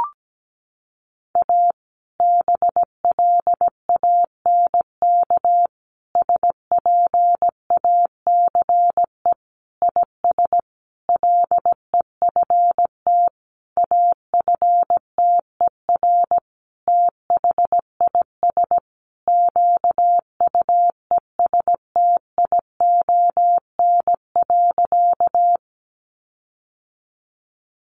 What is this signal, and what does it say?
Morse code, heard as a keyed tone, about 17 wpm, 700 Hz. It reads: A BLANK SPACE IS LEFT AFTER THIS QUESTION.
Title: Sentences from Top 1500 Words 17wpm